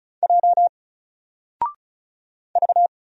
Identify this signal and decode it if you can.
Morse code, heard as a keyed tone, about 36 wpm, 700 Hz. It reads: J V
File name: Single-Letter-Number-35wpm